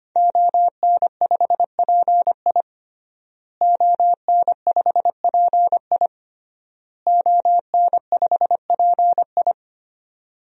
Morse code, heard as a keyed tone, about 25 wpm, 700 Hz. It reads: ON5PI ON5PI ON5PI
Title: DXCC-100 Call Signs - 5 Characters - Repeated 3x - 25wpm